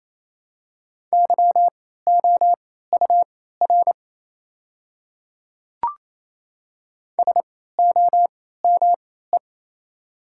Morse code, heard as a keyed tone, about 28 wpm, spaced as for 16 wpm, 700 Hz. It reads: YOUR SOME
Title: 3x Character Spacing - Top 100 Words - 28wpm